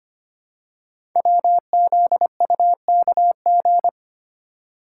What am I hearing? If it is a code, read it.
Morse code, heard as a keyed tone, about 25 wpm, 700 Hz. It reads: WZUKG
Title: Code Groups - 5 Letters - 25wpm